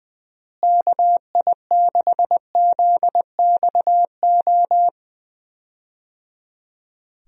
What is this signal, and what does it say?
Morse code, heard as a keyed tone, about 20 wpm, 700 Hz. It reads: KI6ZXO
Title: USA Call Signs - 6 Characters - Repeated 3x - 20wpm